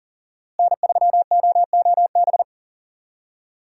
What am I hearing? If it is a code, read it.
Morse code, heard as a keyed tone, about 40 wpm, 700 Hz. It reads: N3OOB
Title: USA Call Signs - 5 Characters - Repeated 3x - 40wpm